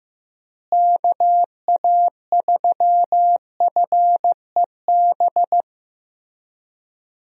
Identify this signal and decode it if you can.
Morse code, heard as a keyed tone, about 15 wpm, 700 Hz. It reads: KA3FEB
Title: USA Call Signs - 6 Characters - Repeated 3x - 15wpm